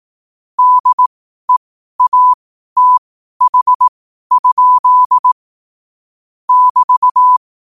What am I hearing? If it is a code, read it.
Morse code, heard as a keyed tone, about 18 wpm, spaced as for 12 wpm, 1000 Hz. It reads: DEATH? <BT>